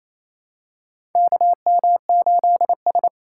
Morse code, heard as a keyed tone, about 28 wpm, 700 Hz. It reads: KM8S
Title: USA Call Signs - 4 Characters - 3x Character Spacing - 28wpm